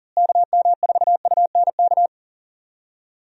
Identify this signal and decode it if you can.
Morse code, heard as a keyed tone, about 40 wpm, 700 Hz. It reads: KM4UNK
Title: USA Call Signs - 6 Characters - Repeated 2x - 40wpm